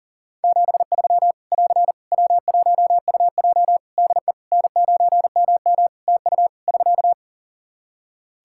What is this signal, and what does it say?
Morse code, heard as a keyed tone, about 38 wpm, 700 Hz. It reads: 73 <AR> W1UJ DE N9MM TU <SK>
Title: Practice QSO - Exchange 4 - Ending - 40wpm